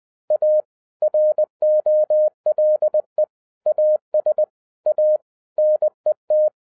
Morse code, heard as a keyed tone, about 20 wpm, 610 Hz. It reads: A ROLE AS A NET
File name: 2012-01-mp3 / n16-THROUGH EIGHT MONTHS-20wpm